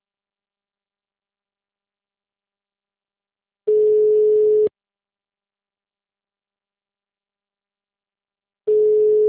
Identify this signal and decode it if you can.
Ringback tone (Europe, 425 Hz).